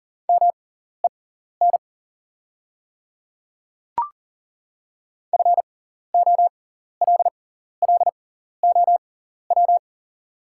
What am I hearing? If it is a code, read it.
Morse code, heard as a keyed tone, about 40 wpm, spaced as for 14 wpm, 690 Hz. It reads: MEN FOLLOW
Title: 6x Character Spacing - Top 300 Words - 40wpm